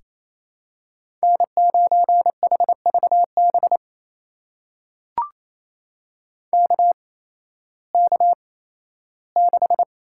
Morse code, heard as a keyed tone, about 28 wpm, spaced as for 15 wpm, 700 Hz. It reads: N9HVB KK6
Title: USA Call Signs - 5 Characters - 8x Character Spacing - 28wpm